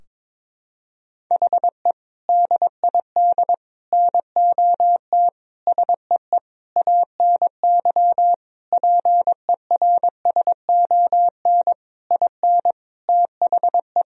Morse code, heard as a keyed tone, about 22 wpm, 700 Hz. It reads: HE DID NOT SEE ANY PERSON IN THE
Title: Sentences from Top 1000 Words - Review Part 3 - 22wpm